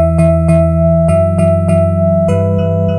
There is no code.